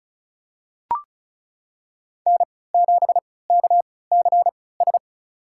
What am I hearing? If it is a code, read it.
Morse code, heard as a keyed tone, about 36 wpm, spaced as for 20 wpm, 690 Hz. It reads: N7KCS
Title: USA Call Signs - 3x Character Spacing - 35wpm